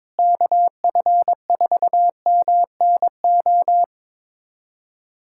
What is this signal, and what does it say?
Morse code, heard as a keyed tone, about 22 wpm, 700 Hz. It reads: KF4MNO
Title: USA Callsign DE Callsign - 6 Characters - 22wpm